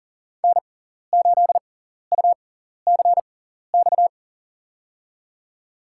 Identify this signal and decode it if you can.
Morse code, heard as a keyed tone, about 40 wpm, spaced as for 14 wpm, 700 Hz. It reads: N8UCX
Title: USA Call Signs - 5 Characters - 6x Character Spacing - 40wpm